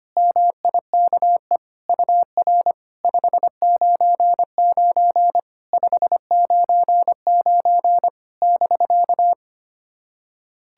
Morse code, heard as a keyed tone, about 25 wpm, 700 Hz. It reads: MIKE UR 599 599 <BK>
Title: SOTA QSO Practice 25wpm